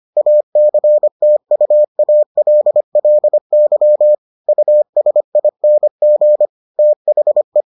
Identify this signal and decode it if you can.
Morse code, heard as a keyed tone, about 25 wpm, 600 Hz. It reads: ACTUALLY USING THE